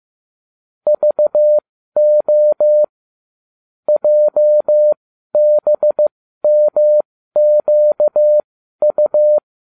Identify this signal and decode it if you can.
Morse code, heard as a keyed tone, about 15 wpm, spaced as for 12 wpm, 600 Hz. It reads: VO JBMQU